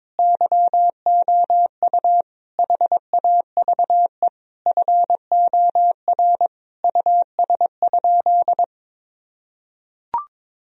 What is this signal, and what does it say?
Morse code, heard as a keyed tone, about 22 wpm, 700 Hz. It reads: YOU HAVE FOR US?
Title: Sentences from Top 1000 Words - Review Part 3 - 22wpm